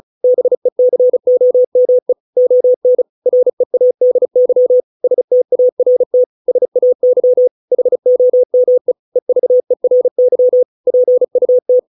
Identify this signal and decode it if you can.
Morse code, heard as a keyed tone, about 34 wpm, 500 Hz. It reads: BECOME ON READY START SAY HOME EVERY PUT